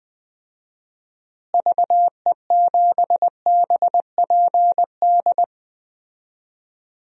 Morse code, heard as a keyed tone, about 20 wpm, 700 Hz. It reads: VE7BPD
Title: DXCC-100 Call Signs - 6 Characters - Repeated 3x - 20wpm